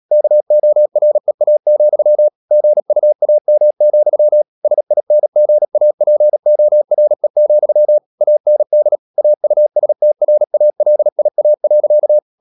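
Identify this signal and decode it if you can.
Morse code, heard as a keyed tone, about 37 wpm, 600 Hz. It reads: KOREA, GUAM, SINGAPORE, AND AUSTRALIA.
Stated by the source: Morse code practice at 35–39 wpm sending English text of Ken Follett's novel Never, from CW med Gnister.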